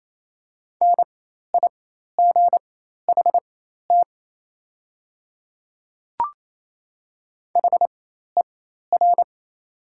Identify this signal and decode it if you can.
Morse code, heard as a keyed tone, about 27 wpm, spaced as for 13 wpm, 700 Hz. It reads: NIGHT HER
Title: 4x Character Spacing - Top 300 Words - 28wpm